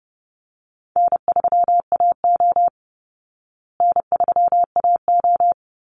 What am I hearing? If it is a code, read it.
Morse code, heard as a keyed tone, about 30 wpm, 700 Hz. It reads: N3AO N3AO
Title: Easy Call Signs Repeated 2x - 30wpm